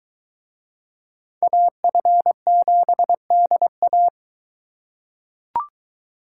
Morse code, heard as a keyed tone, about 23 wpm, 700 Hz. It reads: AF7DA